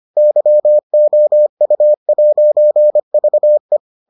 Morse code, heard as a keyed tone, about 25 wpm, 600 Hz. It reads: YOU'VE